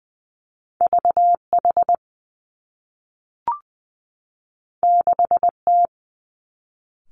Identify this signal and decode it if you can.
Morse code, heard as a keyed tone, about 20 wpm, 700 Hz. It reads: V H 6 T
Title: Two Challenging Letter-Number Combinations 20wpm